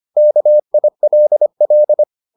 Morse code, heard as a keyed tone, about 25 wpm, 600 Hz. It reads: KILL